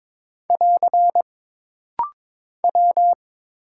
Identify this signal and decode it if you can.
Morse code, heard as a keyed tone, about 22 wpm, 700 Hz. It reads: <AR> W
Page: This rapid-fire timing is geared to push your ICR (Instant Character Recognition) skills.